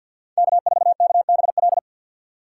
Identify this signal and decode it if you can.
Morse code, heard as a keyed tone, about 48 wpm, 700 Hz. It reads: K4KBL
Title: K1USN SST QSO Practice - 50wpm